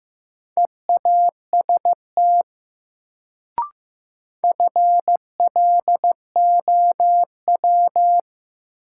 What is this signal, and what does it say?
Morse code, heard as a keyed tone, about 15 wpm, 700 Hz. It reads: EAST FLOW
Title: Sentences from Top 1200 Words - New Word Drill Alt - 15wpm